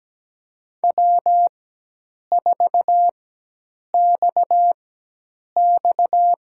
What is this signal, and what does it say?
Morse code, heard as a keyed tone, about 17 wpm, spaced as for 8 wpm, 700 Hz. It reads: W4XX
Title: USA Call Signs - 4 Characters - 4x Character Spacing - 17wpm